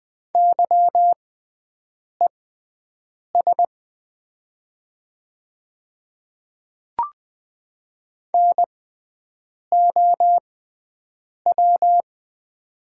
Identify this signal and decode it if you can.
Morse code, heard as a keyed tone, about 20 wpm, spaced as for 7 wpm, 700 Hz. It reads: YES NOW
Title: 6x Character Spacing - Top 500 Words - 20wpm